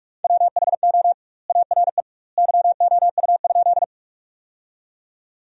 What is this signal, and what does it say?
Morse code, heard as a keyed tone, about 46 wpm, 700 Hz. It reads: WHO ARE YOU?
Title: 2x Word-Spacing - Sentences from Top 100 Words - 45wpm